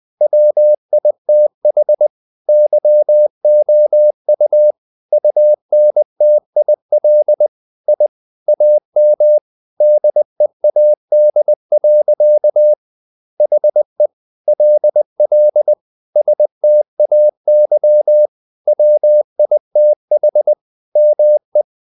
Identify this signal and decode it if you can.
Morse code, heard as a keyed tone, about 20 wpm, 600 Hz. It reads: WITH YOU UNTIL I AM DEAD. HE LL STAY WITH ME
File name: the old man and the sea / the old man and the sea part 1 20wpm